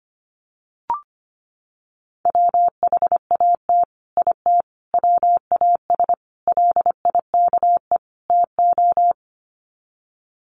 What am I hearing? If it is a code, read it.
Morse code, heard as a keyed tone, about 25 wpm, 700 Hz. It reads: WHAT IT WAS LIKE TO
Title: Sets of 5 Words Encore 25wpm